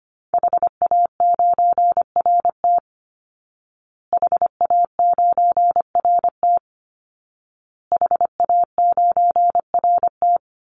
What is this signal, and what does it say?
Morse code, heard as a keyed tone, about 25 wpm, 700 Hz. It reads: HA9RT HA9RT HA9RT
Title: Call Signs Repeated 2x - 25wpm